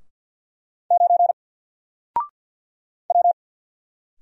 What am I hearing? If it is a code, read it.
Morse code, heard as a keyed tone, about 48 wpm, 700 Hz. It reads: 9 W
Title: Single Character (Letter:Number:Punctuation:Prosign) Alt 50wpm